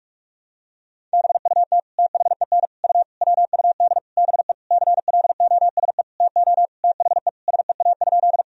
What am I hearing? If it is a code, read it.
Morse code, heard as a keyed tone, about 46 wpm, 700 Hz. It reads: BUT THEN U WUD BE CLOSE TO THE SEA?
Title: Sentences from Top 300 Words and Abbreviations 45wpm